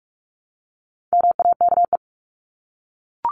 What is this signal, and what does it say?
Morse code, heard as a keyed tone, about 44 wpm, 700 Hz. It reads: MAKE
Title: Top 500 Words 45wpm